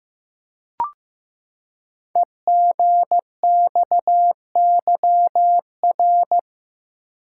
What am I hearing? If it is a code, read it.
Morse code, heard as a keyed tone, about 15 wpm, 700 Hz. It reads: EGXYR
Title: Code Groups - 5 Letters - 15wpm